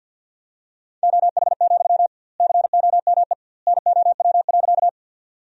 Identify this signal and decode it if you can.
Morse code, heard as a keyed tone, about 51 wpm, 700 Hz. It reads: OH, COME NOW.
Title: 2x Word-Spacing - Sentences from Top 500 Words - 50wpm